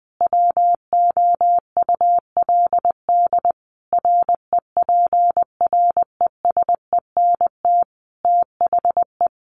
Morse code, heard as a keyed tone, about 20 wpm, 700 Hz. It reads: WOULD REPRESENT THE
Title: Sentences from Top 1200 Words 20wpm